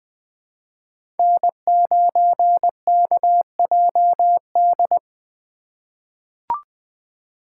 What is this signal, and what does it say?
Morse code, heard as a keyed tone, about 20 wpm, 700 Hz. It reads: N9KJD
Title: USA Call Signs - 5 Characters - 2x Character Spacing - 20wpm